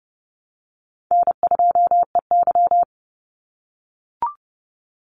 Morse code, heard as a keyed tone, about 30 wpm, 700 Hz. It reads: N2EY
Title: Easy Call Signs - 30wpm